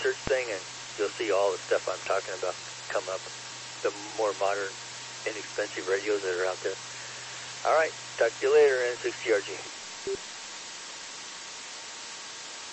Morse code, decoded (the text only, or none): none